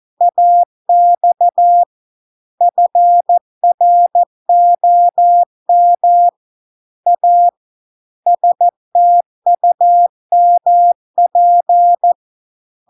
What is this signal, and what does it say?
Morse code, heard as a keyed tone, about 14 wpm, 700 Hz. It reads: AX FROM A STUMP